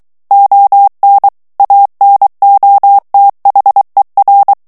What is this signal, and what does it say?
Morse code, heard as a keyed tone, about 23 wpm, 800 Hz. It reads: ON ANOTHER
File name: ANGLAIS texte clair 3 vitesse 22 mots minute